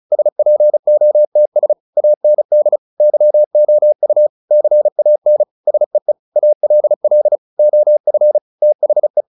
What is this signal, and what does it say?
Morse code, heard as a keyed tone, about 34 wpm, 600 Hz. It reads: SPOTS AND YOU CAN SEE ALL OF THE
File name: the old man and the sea 2 35wpm